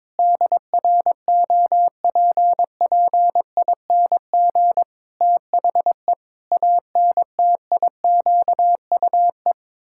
Morse code, heard as a keyed tone, about 22 wpm, 700 Hz. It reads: DROPPING THE ANTIQUE
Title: Pangram Challenge v2 - 22wpm